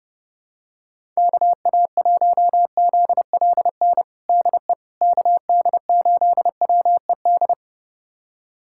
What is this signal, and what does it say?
Morse code, heard as a keyed tone, about 32 wpm, 700 Hz. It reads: KA1ZLN DE KD8WED
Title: USA Callsign DE Callsign - 6 Characters - 30wpm